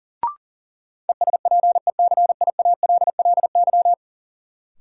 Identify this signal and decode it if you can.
Morse code, heard as a keyed tone, about 40 wpm, 700 Hz. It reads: ESPECIALLY
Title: Sentences from Top 700 Words - New Word Drill - 40wpm